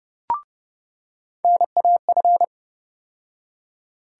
Morse code, heard as a keyed tone, about 30 wpm, 700 Hz. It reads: NAF
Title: USA Call Sign Postfixes - 3 Letters 30wpm